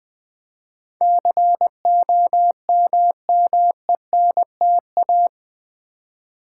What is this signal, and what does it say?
Morse code, heard as a keyed tone, about 20 wpm, 700 Hz. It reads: COMMENTA
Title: Build A Word - 4 Letters At A Time - 11-Letter Words - 20wpm